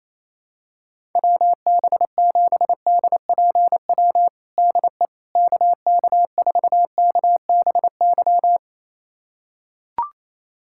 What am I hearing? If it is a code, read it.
Morse code, heard as a keyed tone, about 27 wpm, 700 Hz. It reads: WB7DPW DE KK4KBY
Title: USA Callsign DE Callsign - 6 Characters - 28wpm